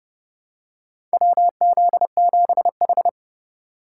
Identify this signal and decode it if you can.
Morse code, heard as a keyed tone, about 30 wpm, 700 Hz. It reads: WZ7H
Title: USA Call Signs - 4 Characters - 4x Character Spacing - 30wpm